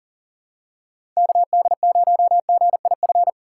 Morse code, heard as a keyed tone, about 39 wpm, 700 Hz. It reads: KD0GIF
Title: USA Call Signs - 6 Characters - 4x Character Spacing - 40wpm